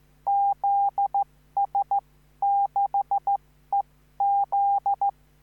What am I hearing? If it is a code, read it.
Morse code, heard as a keyed tone, about 14 wpm, spaced as for 12 wpm, 800 Hz. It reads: ZS6EZ
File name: CQvonZS6EZ